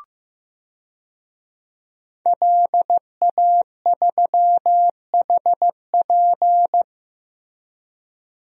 Morse code, heard as a keyed tone, about 15 wpm, 700 Hz. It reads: LA3HP
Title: DXCC-100 Call Signs - 5 Characters - 15wpm